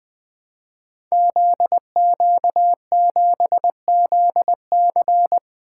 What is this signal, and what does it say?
Morse code, heard as a keyed tone, about 20 wpm, 700 Hz. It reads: ZQ7ZC
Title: Code Groups - 5 Letter:Numbers - 20wpm